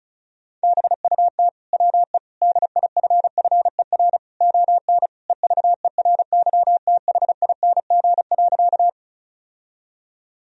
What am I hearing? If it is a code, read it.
Morse code, heard as a keyed tone, about 36 wpm, 700 Hz. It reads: BUT WE DIFFER ON EVERYTHING.
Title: Sentences from Top 1000 Words - Review Part 1 - 35wpm